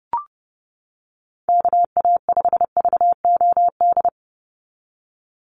Morse code, heard as a keyed tone, about 30 wpm, 700 Hz. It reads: KA5VOD
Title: USA Call Signs 30wpm